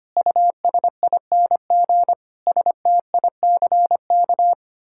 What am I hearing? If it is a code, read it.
Morse code, heard as a keyed tone, about 25 wpm, 700 Hz. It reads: USING STICK
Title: Pangram Challenge v2 - 25wpm